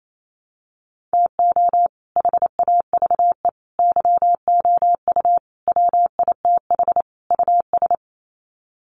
Morse code, heard as a keyed tone, about 28 wpm, 700 Hz. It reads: TO HAVE YOU WITH US
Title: Sets of 5 Words 28wpm